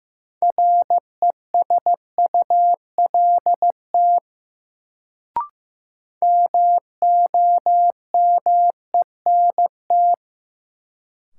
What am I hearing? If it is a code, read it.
Morse code, heard as a keyed tone, about 15 wpm, 700 Hz. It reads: RESULT MOMENT
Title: Sets of 3 Words - New Word Drill - 15wpm